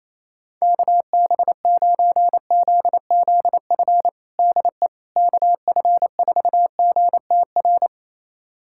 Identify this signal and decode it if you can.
Morse code, heard as a keyed tone, about 28 wpm, 700 Hz. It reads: KB9ZZF DE KF4GTR
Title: USA Callsign DE Callsign - 6 Characters - 28wpm